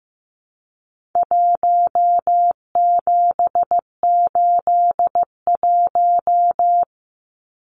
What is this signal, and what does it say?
Morse code, heard as a keyed tone, about 15 wpm, 700 Hz. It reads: 1781
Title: Numbers - Contest Style - 4-Digit 15wpm